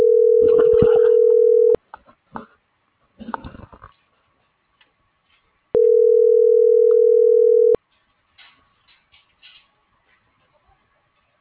Ringback tone (North America), 440 + 480 Hz.